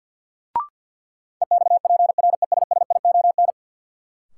Morse code, heard as a keyed tone, about 52 wpm, 700 Hz. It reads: EXPRESSION
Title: Sentences from Top 1200 Words - New Word Drill - 50wpm